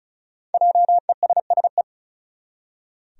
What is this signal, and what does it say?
Morse code, heard as a keyed tone, about 34 wpm, 700 Hz. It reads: JESSE
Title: USA Common Names - Max 5 Letters - 35wpm